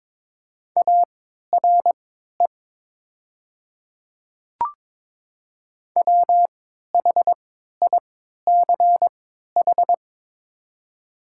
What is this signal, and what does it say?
Morse code, heard as a keyed tone, about 22 wpm, spaced as for 12 wpm, 700 Hz. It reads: ARE WHICH
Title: 3x Character Spacing - Top 300 Words - 22wpm